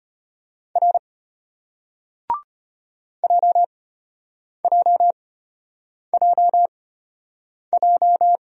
Morse code, heard as a keyed tone, about 33 wpm, spaced as for 8 wpm, 690 Hz. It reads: R JJJJ